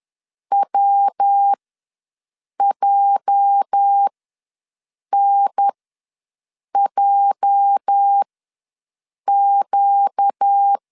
Morse code, heard as a keyed tone, about 10 wpm, spaced as for 6 wpm, 800 Hz. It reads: WJNJQ
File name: LEZIONI / 8 YJ45 / 8 YJ ws 10wpm 4043